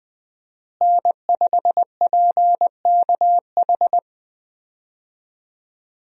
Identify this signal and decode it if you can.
Morse code, heard as a keyed tone, about 21 wpm, 700 Hz. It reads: N5PKH
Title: USA Call Signs - 5 Characters - Repeated 2x - 20wpm